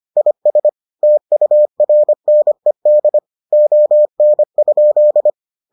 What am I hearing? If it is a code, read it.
Morse code, heard as a keyed tone, about 25 wpm, 600 Hz. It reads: IS TURNED ON?